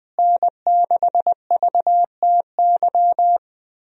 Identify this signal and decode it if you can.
Morse code, heard as a keyed tone, about 21 wpm, 700 Hz. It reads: N6VTY
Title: USA Call Signs - 5 Characters - Repeated 3x - 20wpm